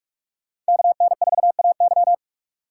Morse code, heard as a keyed tone, about 57 wpm, 700 Hz. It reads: KN4AY